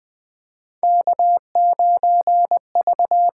Morse code, heard as a keyed tone, about 20 wpm, 700 Hz. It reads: K9V
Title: USA Call Signs - 5 Characters - 8x Character Spacing - 20wpm